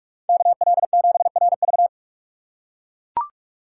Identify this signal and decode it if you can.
Morse code, heard as a keyed tone, about 44 wpm, 700 Hz. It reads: KL7RV
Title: USA Call Signs - 5 Characters - 6x Character Spacing - 45wpm